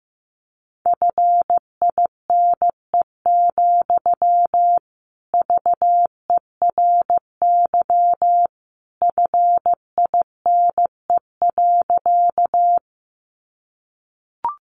Morse code, heard as a keyed tone, about 15 wpm, 700 Hz. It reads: FINE, VERY FINE.
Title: Sentences from Top 500 Words 15wpm